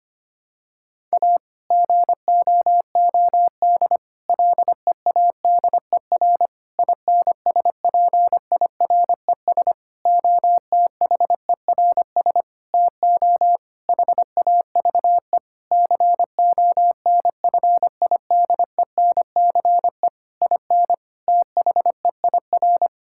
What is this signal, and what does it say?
Morse code, heard as a keyed tone, about 25 wpm, 700 Hz. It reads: A GOOD LEADER INSPIRES OTHERS TO HAVE CONFIDENCE IN THEIR
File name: practice / 25wpm / Difficult Sentences with Hints - 25wpm